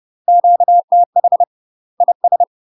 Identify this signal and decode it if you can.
Morse code, heard as a keyed tone, about 30 wpm, 700 Hz. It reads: QTH IS
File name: UK MASTER QSOs TRAINING / 30WPM / UK MASTER QSOs 0046